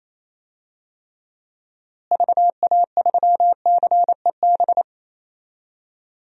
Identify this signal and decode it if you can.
Morse code, heard as a keyed tone, about 28 wpm, 700 Hz. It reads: VA3CEB
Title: DXCC-100 Call Signs - 6 Characters - Repeated 2x - 28wpm